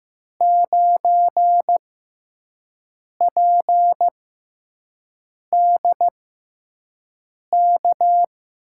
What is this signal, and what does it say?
Morse code, heard as a keyed tone, about 15 wpm, spaced as for 5 wpm, 700 Hz. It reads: 9PDK
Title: USA Call Signs - 6x Character Spacing - 15wpm